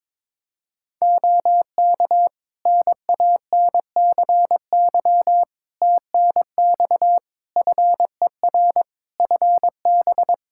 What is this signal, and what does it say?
Morse code, heard as a keyed tone, about 22 wpm, 700 Hz. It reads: OK NANCY TNX FER FB
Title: Practice QSO - Exchange 4 - Ending - 22wpm